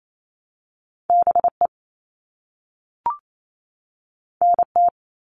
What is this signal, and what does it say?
Morse code, heard as a keyed tone, about 28 wpm, 700 Hz. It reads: BE NT
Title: All Cut Numbers - 2-Digit 28wpm